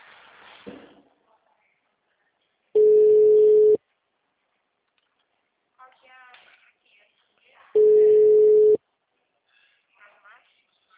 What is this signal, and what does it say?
Ringback tone (Europe, 425 Hz), with speech.
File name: out- -2012- - - .98